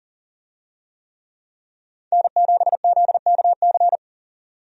Morse code, heard as a keyed tone, about 40 wpm, 700 Hz. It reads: N7ZKC
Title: USA Call Signs - 5 Characters - 40wpm